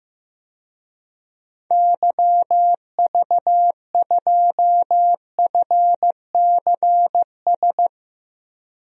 Morse code, heard as a keyed tone, about 15 wpm, 700 Hz. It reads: YV2FCS
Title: DXCC-100 Call Signs - 6 Characters - Repeated 2x - 15wpm